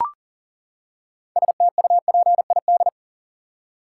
Morse code, heard as a keyed tone, about 40 wpm, 700 Hz. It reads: STUPID
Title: Six Letter Words - 40wpm